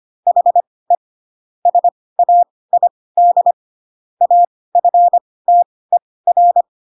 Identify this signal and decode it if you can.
Morse code, heard as a keyed tone, about 25 wpm, spaced as for 18 wpm, 700 Hz. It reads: HE SAID AFTER
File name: Pinocchio / Pinocchio ch7 12 25 ik1hge